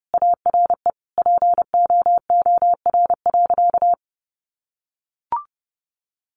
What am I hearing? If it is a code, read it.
Morse code, heard as a keyed tone, about 30 wpm, 700 Hz. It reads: ARE POOR.
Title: Sentences from Top 1100 Words 30wpm